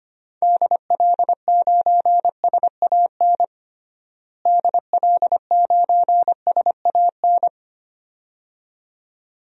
Morse code, heard as a keyed tone, about 25 wpm, 700 Hz. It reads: DL9SAN DL9SAN
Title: DXCC-100 Call Signs - 6 Characters - Repeated 2x - 25wpm